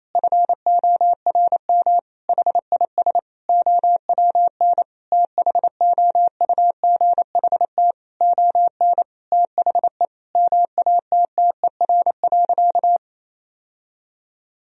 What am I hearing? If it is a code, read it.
Morse code, heard as a keyed tone, about 28 wpm, 700 Hz. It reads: FORM HIS OWN THOUGHT ON THE MATTER.
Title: Sentences from Top 1000 Words - Review Part 1 - 28wpm